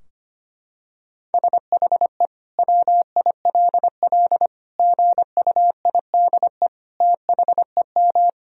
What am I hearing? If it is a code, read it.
Morse code, heard as a keyed tone, about 25 wpm, 700 Hz. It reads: SHE WILL GUIDE THEM
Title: Sentences from Top 1000 Words - Review Part 3 - 25wpm